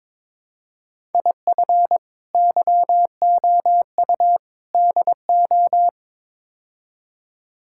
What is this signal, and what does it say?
Morse code, heard as a keyed tone, about 22 wpm, 700 Hz. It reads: IF YOU DO
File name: practice / 22wpm / Build A Phrase - 5-Word Phrases - 22wpm